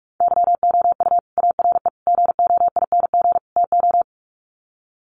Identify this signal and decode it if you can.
Morse code, heard as a keyed tone, about 44 wpm, 700 Hz. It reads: YOU ARE GOING TO
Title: Sets of 4 Words 45wpm